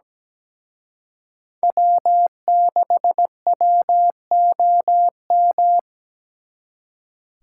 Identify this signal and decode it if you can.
Morse code, heard as a keyed tone, about 22 wpm, 700 Hz. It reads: W6WOM